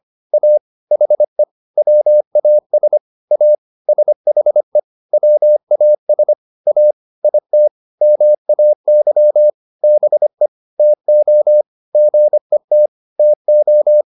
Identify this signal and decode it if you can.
Morse code, heard as a keyed tone, about 25 wpm, 600 Hz. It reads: A HE WAS A SHE WAS A IT MAY BE TO GET TO